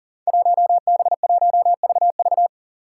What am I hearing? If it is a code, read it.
Morse code, heard as a keyed tone, about 41 wpm, 700 Hz. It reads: 1B1VV
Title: Five Challenging Letter-Number Combinations 40wpm